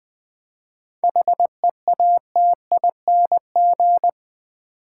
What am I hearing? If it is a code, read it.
Morse code, heard as a keyed tone, about 20 wpm, 700 Hz. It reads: HEATING
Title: 7 Letter Words - First 3 Letters Given - 20wpm